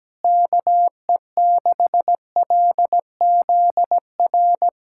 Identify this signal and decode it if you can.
Morse code, heard as a keyed tone, about 17 wpm, 700 Hz. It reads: KE6LZR
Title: USA Callsign DE Callsign - 6 Characters - 17wpm